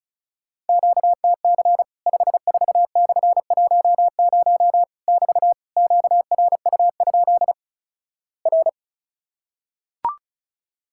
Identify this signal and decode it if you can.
Morse code, heard as a keyed tone, about 36 wpm, 700 Hz. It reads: QTC 54/10 <BT> QRU? R
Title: Worked All Europe DX Contest - QTC Exchange - 35wpm